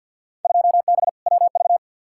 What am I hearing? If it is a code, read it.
Morse code, heard as a keyed tone, about 52 wpm, 700 Hz. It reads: 2B WV
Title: Field Day Exchanges Encore - 50wpm